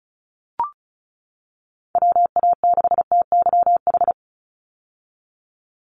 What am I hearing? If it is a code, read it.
Morse code, heard as a keyed tone, about 36 wpm, 700 Hz. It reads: WA6TYH
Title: USA Call Signs 35wpm